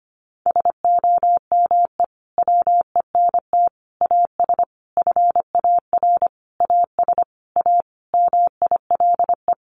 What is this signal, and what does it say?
Morse code, heard as a keyed tone, about 25 wpm, 700 Hz. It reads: SOME WENT AS FAR AS A MILE
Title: Sentences from Top 1100 Words 25wpm